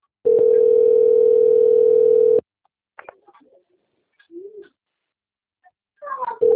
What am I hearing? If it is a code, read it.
Ringback tone (North America), 440 + 480 Hz.
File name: out- - - - - .72660